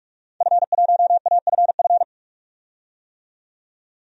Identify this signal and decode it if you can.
Morse code, heard as a keyed tone, about 46 wpm, 700 Hz. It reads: F1AFF
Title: DXCC-100 Call Signs - 5 Characters - 45wpm